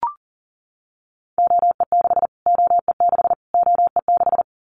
Morse code, heard as a keyed tone, about 40 wpm, 700 Hz. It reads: OE6 OE6 OE6
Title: Callsign Prefixes for Top 25 DXCC - Repeated 3x - 40wpm